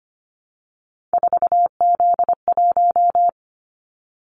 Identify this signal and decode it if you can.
Morse code, heard as a keyed tone, about 25 wpm, 690 Hz. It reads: 4Z1
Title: Callsign Prefixes for Top 50 DXCC - 25wpm